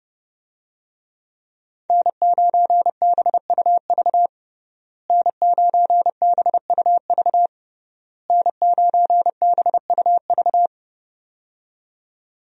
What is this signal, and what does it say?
Morse code, heard as a keyed tone, about 30 wpm, 700 Hz. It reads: N9BUV N9BUV N9BUV
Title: USA Call Signs - 5 Characters - Repeated 3x - 30wpm